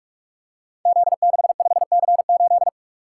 Morse code, heard as a keyed tone, about 44 wpm, 700 Hz. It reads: Z65C8
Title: Code Groups - 5 Letter:Numbers - 45wpm